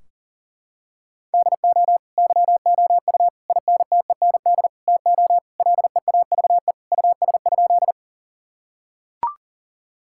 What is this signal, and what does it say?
Morse code, heard as a keyed tone, about 40 wpm, 700 Hz. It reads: DO YOU INTEND TO LEAVE US?
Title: Sentences from Top 1600 Words 40wpm